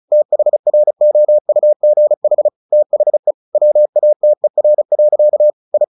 Morse code, heard as a keyed tone, about 34 wpm, 600 Hz. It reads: THROUGH THE WATER. I
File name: the old man and the sea / the old man and the sea 1 35wpm